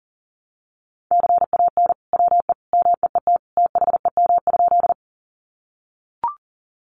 Morse code, heard as a keyed tone, about 37 wpm, 700 Hz. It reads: CAN WE MEET THEM?
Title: Sentences from Top 1100 Words 40wpm